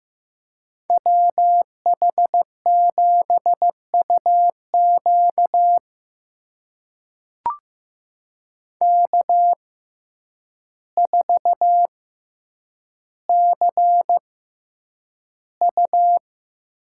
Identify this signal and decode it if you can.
Morse code, heard as a keyed tone, about 16 wpm, spaced as for 8 wpm, 700 Hz. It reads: WH7UQ K4CU
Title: USA Call Signs - 6x Character Spacing - 15wpm